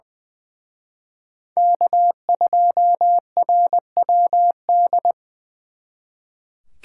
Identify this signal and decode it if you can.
Morse code, heard as a keyed tone, about 23 wpm, 700 Hz. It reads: K2RWD
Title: Speed Racing - USA 1x3 Call Signs 20wpm